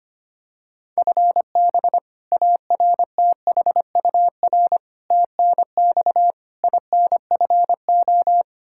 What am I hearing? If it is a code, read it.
Morse code, heard as a keyed tone, about 25 wpm, 700 Hz. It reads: FB ARTHUR TNX INFO
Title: Practice QSO - Exchange 3 - More Info - 25wpm